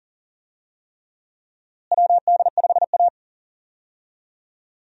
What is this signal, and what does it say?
Morse code, heard as a keyed tone, about 37 wpm, 700 Hz. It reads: WD5A
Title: USA Call Signs - 4 Characters - Repeated 3x - 40wpm